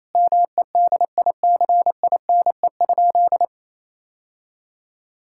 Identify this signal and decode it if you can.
Morse code, heard as a keyed tone, about 28 wpm, 700 Hz. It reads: MEDICINE?
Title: Sentences from Top 1000 Words - Review Part 6 - 28wpm